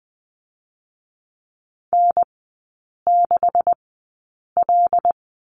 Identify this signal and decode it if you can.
Morse code, heard as a keyed tone, about 20 wpm, spaced as for 8 wpm, 700 Hz. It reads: N6L
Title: 127 Lesson 21e - Number 6 - Three Characters 20wpm